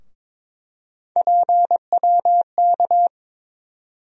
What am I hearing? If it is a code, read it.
Morse code, heard as a keyed tone, about 22 wpm, 700 Hz. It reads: PWK
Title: USA Call Sign Postfixes - 3 Letters 22wpm